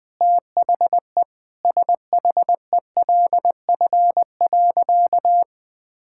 Morse code, heard as a keyed tone, about 20 wpm, 700 Hz. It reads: THE SHELF.
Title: Sentences with 7-Word Phrases - 20wpm